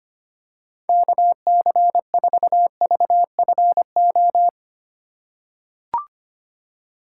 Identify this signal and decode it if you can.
Morse code, heard as a keyed tone, about 25 wpm, 700 Hz. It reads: KC4VFO
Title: USA Call Signs - 6 Characters - 2x Character Spacing - 25wpm